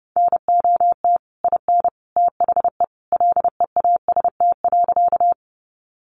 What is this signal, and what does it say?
Morse code, heard as a keyed tone, about 28 wpm, 700 Hz. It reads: NOT IN THE LEAST.
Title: Sentences from Top 700 Words 30wpm